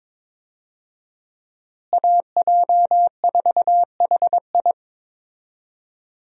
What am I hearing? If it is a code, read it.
Morse code, heard as a keyed tone, about 22 wpm, 700 Hz. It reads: AJ4HI
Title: USA Call Signs - 5 Characters - Repeated 3x - 22wpm